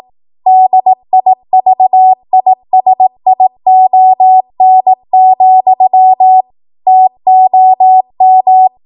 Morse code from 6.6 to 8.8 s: TOM